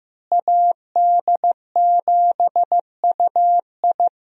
Morse code, heard as a keyed tone, about 15 wpm, 700 Hz. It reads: AD7UI
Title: USA Call Signs - 5 Characters - Repeated 3x - 15wpm